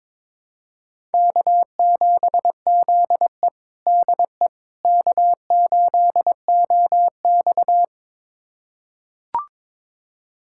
Morse code, heard as a keyed tone, about 22 wpm, 700 Hz. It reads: K7ZE DE K8OX
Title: USA Callsign DE Callsign - 4 Characters - 22wpm